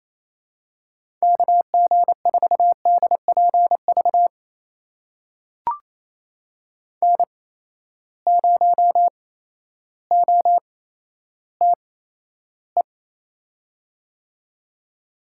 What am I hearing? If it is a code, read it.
Morse code, heard as a keyed tone, about 28 wpm, spaced as for 13 wpm, 700 Hz. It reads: KG4DPV N0OTE
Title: USA Call Signs - 8x Character Spacing - 28wpm